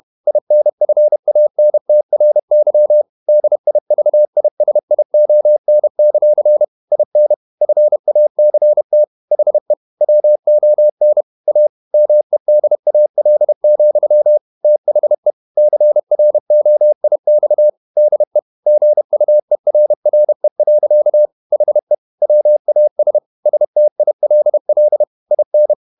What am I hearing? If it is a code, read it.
Morse code, heard as a keyed tone, about 31 wpm, 600 Hz. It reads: INFANTRY DIVISION; IN FACT HE WON A MEDAL, THE CROIX DE GUERRE. HE WAS STILL IN
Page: Never 31wpm | CW med Gnister